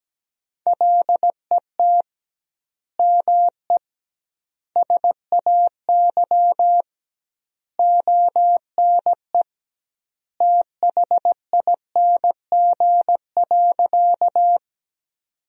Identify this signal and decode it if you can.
Morse code, heard as a keyed tone, about 17 wpm, 700 Hz. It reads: LET ME SAY ONE THING.
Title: 2x Word-Spacing - Sentences from Top 300 Words - 17wpm